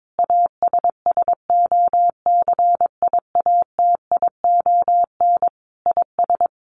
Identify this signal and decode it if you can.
Morse code, heard as a keyed tone, about 22 wpm, 700 Hz. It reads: ASSOCIATION IS
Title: Sentences from Top 1100 Words 22wpm